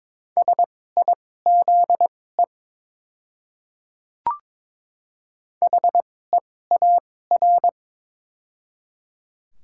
Morse code, heard as a keyed tone, about 22 wpm, spaced as for 16 wpm, 700 Hz. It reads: SIZE HEAR
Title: 2x Character Spacing - Top 500 Words - 22wpm